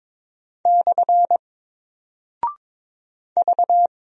Morse code, heard as a keyed tone, about 22 wpm, 700 Hz. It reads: / V
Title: Single Character (Letter:Number:Punctuation:Prosign) Alt 22wpm